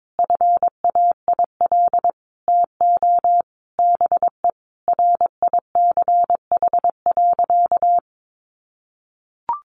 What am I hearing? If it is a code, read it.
Morse code, heard as a keyed tone, about 22 wpm, 700 Hz. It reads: FAIL TO BE RICH.
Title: Sentences from Top 1100 Words 22wpm